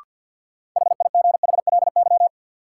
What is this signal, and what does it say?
Morse code, heard as a keyed tone, about 52 wpm, 700 Hz. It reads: HIGHLY
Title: Sentences from Top 1100 Words - New Word Drill Alt - 50wpm